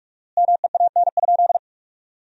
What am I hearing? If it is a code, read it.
Morse code, heard as a keyed tone, about 47 wpm, 700 Hz. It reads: MEAN?
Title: 4x Word-Spacing - Sentences from Top 200 Words - 45wpm